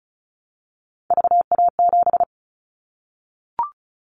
Morse code, heard as a keyed tone, about 36 wpm, 700 Hz. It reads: VA7
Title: Callsign Prefixes for Top 50 DXCC - 35wpm